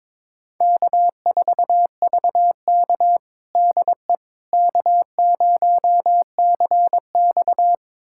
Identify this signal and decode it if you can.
Morse code, heard as a keyed tone, about 21 wpm, 700 Hz. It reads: K4VK DE K0CX
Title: USA Callsign DE Callsign - 4 Characters - 22wpm